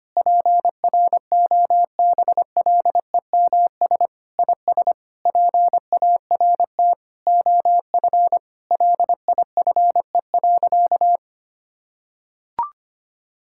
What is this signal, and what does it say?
Morse code, heard as a keyed tone, about 25 wpm, 700 Hz. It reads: PROBLEMS IS PART OF LIFE.
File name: practice / 25wpm / Sentences with 3-Word Phrases - 25wpm